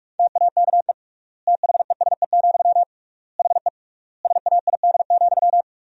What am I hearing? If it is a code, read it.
Morse code, heard as a keyed tone, about 47 wpm, 700 Hz. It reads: TAKE THESE, HE SAID,
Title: 3x Word-Spacing - Sentences from Top 300 Words - 45wpm